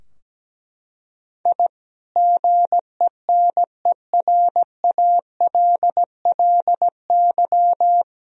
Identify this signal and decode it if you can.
Morse code, heard as a keyed tone, about 17 wpm, 700 Hz. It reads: I GENERALLY
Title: Sentences from Top 1000 Words - Review Part 4 - 17wpm